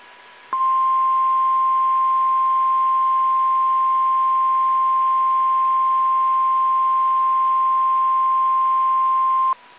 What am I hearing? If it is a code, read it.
Emergency Alert System (SAME) tones (the 1050 Hz attention tone). It is heard from 0.5 to 9.6 s.
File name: eas / KSDZKDJL / KSDZKDJL